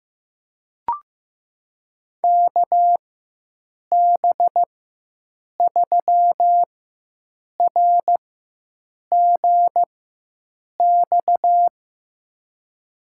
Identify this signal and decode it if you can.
Morse code, heard as a keyed tone, about 15 wpm, spaced as for 7 wpm, 700 Hz. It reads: KB3RGX
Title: USA Call Signs - 6 Characters - 4x Character Spacing - 15wpm